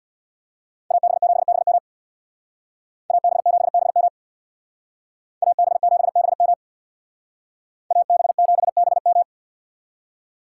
Morse code, heard as a keyed tone, about 64 wpm, 700 Hz. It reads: AB7BM AB7BM AB7BM AB7BM